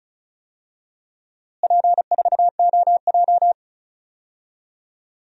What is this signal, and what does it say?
Morse code, heard as a keyed tone, about 34 wpm, 700 Hz. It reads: P4OJ
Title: DXCC-100 Call Signs - 4 Characters - Repeated 2x - 35wpm